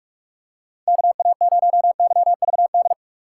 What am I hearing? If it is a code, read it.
Morse code, heard as a keyed tone, about 46 wpm, 700 Hz. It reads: KA0YVD
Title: USA Call Signs - 3x Character Spacing - 45wpm